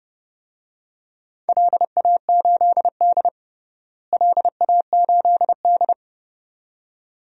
Morse code, heard as a keyed tone, about 31 wpm, 700 Hz. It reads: LA8D LA8D
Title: DXCC-100 Call Signs - 4 Characters - Repeated 2x - 30wpm